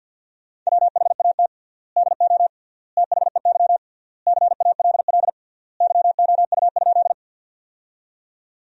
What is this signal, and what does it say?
Morse code, heard as a keyed tone, about 50 wpm, 700 Hz. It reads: WHAT DO THEY CALL YOU?
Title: 3x Word-Spacing - Sentences from Top 100 Words - 50wpm